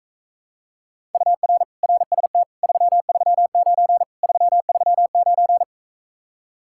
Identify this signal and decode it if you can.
Morse code, heard as a keyed tone, about 44 wpm, 700 Hz. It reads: UR RST 339 339